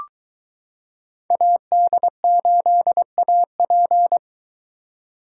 Morse code, heard as a keyed tone, about 22 wpm, 700 Hz. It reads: AD8AP